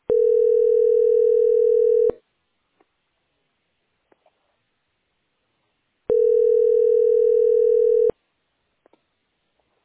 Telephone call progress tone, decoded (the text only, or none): ringback tone